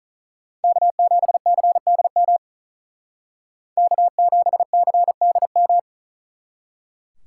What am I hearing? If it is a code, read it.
Morse code, heard as a keyed tone, about 46 wpm, 700 Hz. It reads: K7CDM K7CDM